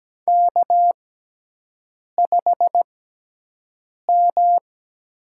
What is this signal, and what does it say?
Morse code, heard as a keyed tone, about 17 wpm, spaced as for 6 wpm, 700 Hz. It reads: K5M
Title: USA Call Signs - 5 Characters - 6x Character Spacing - 17wpm